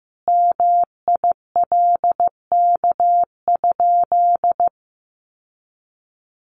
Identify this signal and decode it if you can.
Morse code, heard as a keyed tone, about 15 wpm, 700 Hz. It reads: MILK?
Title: Sentences from Top 1500 Words 15wpm